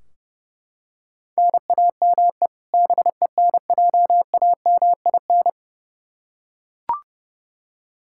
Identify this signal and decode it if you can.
Morse code, heard as a keyed tone, about 32 wpm, 700 Hz. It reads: NAME BENJAMIN
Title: K1USN SST QSO Practice 30wpm